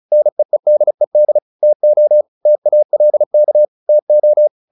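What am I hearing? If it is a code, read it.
Morse code, heard as a keyed tone, about 34 wpm, 600 Hz. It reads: NEEDED TO TALK TO
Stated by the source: Morse practice at 35 wpm from CW med Gnister, sending Ken Follett's novel Never in English.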